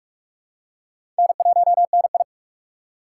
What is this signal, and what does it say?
Morse code, heard as a keyed tone, about 46 wpm, 700 Hz. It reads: N1NI
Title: USA Call Signs - 4 Characters - 2x Character Spacing - 45wpm